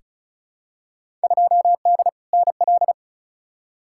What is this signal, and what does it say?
Morse code, heard as a keyed tone, about 36 wpm, 700 Hz. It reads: 2D NL
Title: Field Day Exchanges Encore - 35wpm